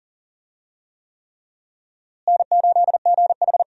Morse code, heard as a keyed tone, about 40 wpm, 700 Hz. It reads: N8GH
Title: USA Call Signs - 4 Characters - Repeated 3x - 40wpm